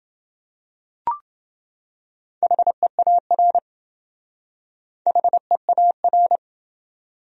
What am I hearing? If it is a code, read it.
Morse code, heard as a keyed tone, about 28 wpm, 700 Hz. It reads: HEAR HEAR